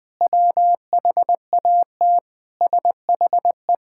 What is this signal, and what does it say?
Morse code, heard as a keyed tone, about 21 wpm, 700 Hz. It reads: WHAT SHE
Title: Sentences from Top 1000 Words - Review Part 6 - 20wpm